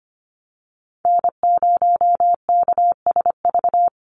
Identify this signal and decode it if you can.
Morse code, heard as a keyed tone, about 25 wpm, 700 Hz. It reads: N0KSV
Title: USA Call Signs Repeated 2x 25wpm